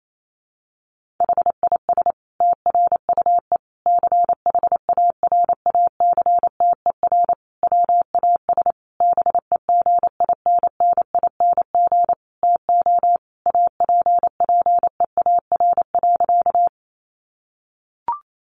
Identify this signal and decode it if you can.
Morse code, heard as a keyed tone, about 28 wpm, 700 Hz. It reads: HIS TRUE CHARACTER WAS BEGINNING TO APPEAR.
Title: Sentences from Top 1200 Words 28wpm